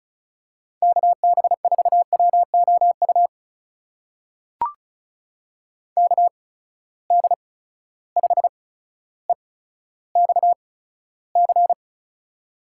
Morse code, heard as a keyed tone, about 36 wpm, spaced as for 15 wpm, 700 Hz. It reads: KB4WOU KD5EXC
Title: USA Call Signs - 6 Characters - 8x Character Spacing - 35wpm